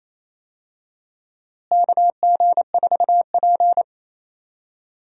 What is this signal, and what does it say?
Morse code, heard as a keyed tone, about 28 wpm, 700 Hz. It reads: KG4P
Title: USA Call Signs - 4 Characters - Repeated 3x - 28wpm